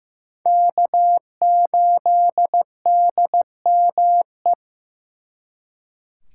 Morse code, heard as a keyed tone, about 18 wpm, 700 Hz. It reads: K8DME
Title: Speed Racing - USA Call Signs 15wpm